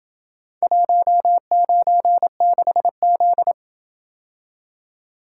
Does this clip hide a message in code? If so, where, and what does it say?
Yes: Morse code from 0.6 to 3.5 s, heard as a keyed tone, about 27 wpm, 700 Hz. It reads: 196Z